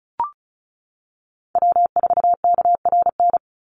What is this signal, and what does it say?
Morse code, heard as a keyed tone, about 34 wpm, 700 Hz. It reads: W4KRN
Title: NA SOTA Chaser Callsigns 35wpm